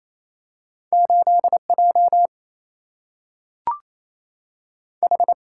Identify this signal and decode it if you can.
Morse code, heard as a keyed tone, about 28 wpm, 700 Hz. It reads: 8 J H
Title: Single-Letter Single-Number 28wpm